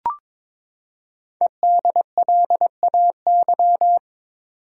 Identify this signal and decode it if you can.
Morse code, heard as a keyed tone, about 22 wpm, 700 Hz. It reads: EDLAY
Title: Code Groups - 5 Letters - 22wpm